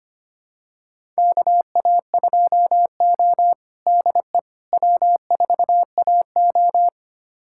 Morse code, heard as a keyed tone, about 25 wpm, 700 Hz. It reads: KA2O DE W4AO
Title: USA Callsign DE Callsign - 4 Characters - 25wpm